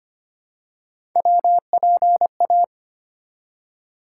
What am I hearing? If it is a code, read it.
Morse code, heard as a keyed tone, about 25 wpm, 700 Hz. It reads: WPA
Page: morse-code-ninja-practice-25wpm